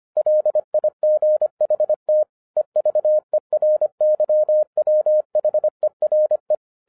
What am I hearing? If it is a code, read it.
Morse code, heard as a keyed tone, about 25 wpm, 610 Hz. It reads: LIGHT EVERYWHERE
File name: 2012-01-mp3 / n14-FUN WITH NUMBERS-25wpm